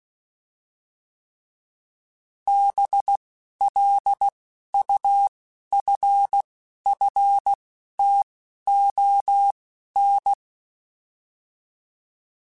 Morse code, heard as a keyed tone, about 16 wpm, spaced as for 11 wpm, 800 Hz. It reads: BLUFFTON